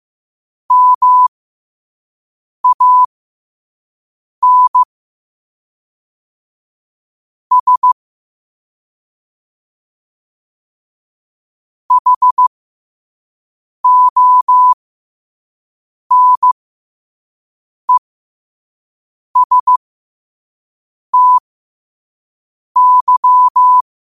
Morse code, heard as a keyed tone, about 15 wpm, spaced as for 5 wpm, 1000 Hz. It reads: MAN S HONESTY
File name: qotdfiles / 5wpm